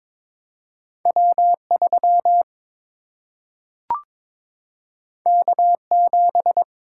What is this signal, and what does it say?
Morse code, heard as a keyed tone, about 22 wpm, 700 Hz. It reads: W 3 K 7
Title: Single-Letter Single-Number 22wpm